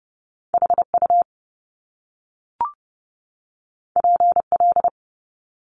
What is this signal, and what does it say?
Morse code, heard as a keyed tone, about 30 wpm, 700 Hz. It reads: HU PL
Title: Top Two Letter Combinations in Words 30wpm